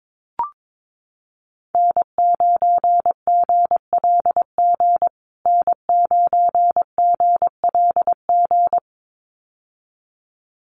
Morse code, heard as a keyed tone, about 22 wpm, 700 Hz. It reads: N9GLG N9GLG
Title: USA Call Signs Repeated 2x 22wpm